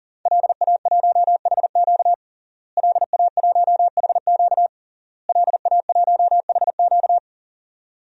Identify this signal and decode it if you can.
Morse code, heard as a keyed tone, about 40 wpm, 700 Hz. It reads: LA1HQ LA1HQ LA1HQ
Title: DXCC-100 Call Signs - 5 Characters - Repeated 3x - 40wpm